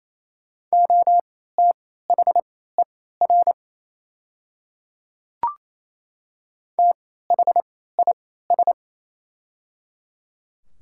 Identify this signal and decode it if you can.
Morse code, heard as a keyed tone, about 27 wpm, spaced as for 16 wpm, 690 Hz. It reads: OTHER THIS
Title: 3x Character Spacing - Top 100 Words - 28wpm